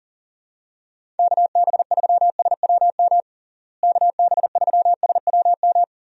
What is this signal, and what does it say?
Morse code, heard as a keyed tone, about 40 wpm, 700 Hz. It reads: KB3SWM KB3SWM
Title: USA Call Signs - 6 Characters - Repeated 3x - 40wpm